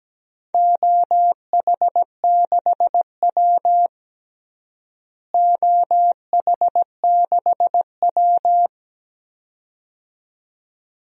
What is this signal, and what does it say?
Morse code, heard as a keyed tone, about 17 wpm, 700 Hz. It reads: OH6W OH6W
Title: DXCC-100 Call Signs - 4 Characters - Repeated 2x - 17wpm